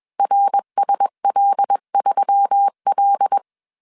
Morse code, heard as a keyed tone, about 21 wpm, 800 Hz. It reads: RSL3L
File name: CW / DET / 1 6 1500C S A 20wpm 2048